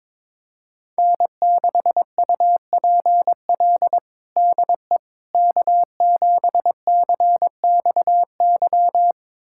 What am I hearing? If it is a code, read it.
Morse code, heard as a keyed tone, about 22 wpm, 700 Hz. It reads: N6UPL DE K7CXY
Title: USA Callsign DE Callsign - 5 Characters - 22wpm